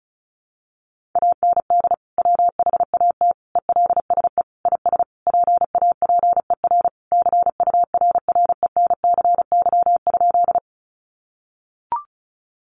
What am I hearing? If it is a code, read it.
Morse code, heard as a keyed tone, about 34 wpm, 700 Hz. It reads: AND WHAT ELSE IS PAPER CURRENCY?
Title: Sentences from Top 1500 Words 35wpm